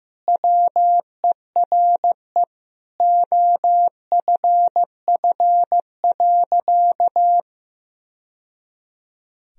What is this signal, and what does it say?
Morse code, heard as a keyed tone, about 15 wpm, 700 Hz. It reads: WERE OFF.
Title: Sentences from Top 1000 Words - Review Part 1 - 15wpm